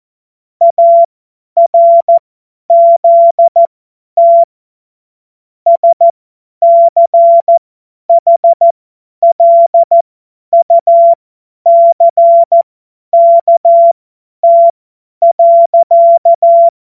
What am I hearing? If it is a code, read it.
Morse code, heard as a keyed tone, about 14 wpm, spaced as for 10 wpm, 680 Hz. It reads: ARZT SCHLUCKT.
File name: Morsekurs / Witze / Witz 12 März 2024 50 F70